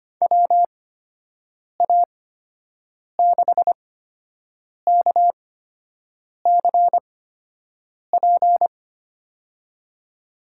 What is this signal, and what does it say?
Morse code, heard as a keyed tone, about 25 wpm, spaced as for 7 wpm, 700 Hz. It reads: WA6KCP
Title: USA Call Signs - 6 Characters - 8x Character Spacing - 25wpm